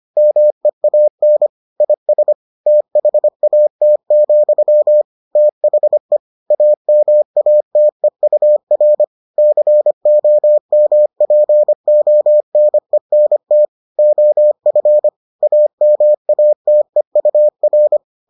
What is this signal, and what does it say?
Morse code, heard as a keyed tone, about 25 wpm, 600 Hz. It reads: MEAN IS THAT, THE AMATEUR COMPONENT OF AMATEUR